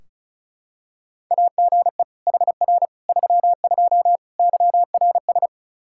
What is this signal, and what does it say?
Morse code, heard as a keyed tone, about 34 wpm, 700 Hz. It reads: AGE HR 32 YRS
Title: Practice QSO - Exchange 3 - More Info - 35wpm